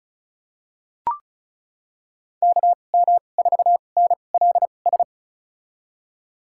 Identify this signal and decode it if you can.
Morse code, heard as a keyed tone, about 36 wpm, spaced as for 25 wpm, 700 Hz. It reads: KM4NLS
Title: USA Call Signs - 6 Characters - 2x Character Spacing - 35wpm